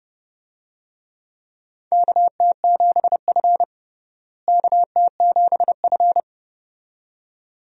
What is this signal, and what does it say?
Morse code, heard as a keyed tone, about 32 wpm, 700 Hz. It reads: KT7F KT7F
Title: USA Call Signs - 4 Characters - Repeated 2x - 30wpm